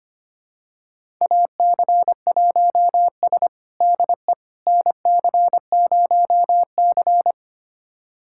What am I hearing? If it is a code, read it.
Morse code, heard as a keyed tone, about 25 wpm, 700 Hz. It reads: AC1S DE NC0C
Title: USA Callsign DE Callsign - 4 Characters - 25wpm